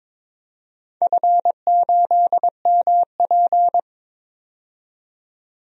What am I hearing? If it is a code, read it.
Morse code, heard as a keyed tone, about 22 wpm, 700 Hz. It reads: F8MP
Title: DXCC-100 Call Signs - 4 Characters - 22wpm